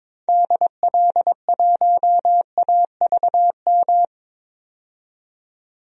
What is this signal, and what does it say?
Morse code, heard as a keyed tone, about 22 wpm, 700 Hz. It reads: DL1AVM
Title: DXCC-100 Call Signs - 6 Characters - 22wpm